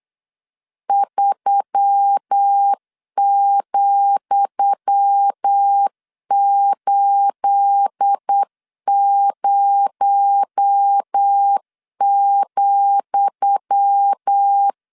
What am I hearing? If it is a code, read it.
Morse code, heard as a keyed tone, about 8 wpm, 800 Hz. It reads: 3,80,
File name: LEZIONI / 10 SIGNS / 10D COMA 8wpm 3749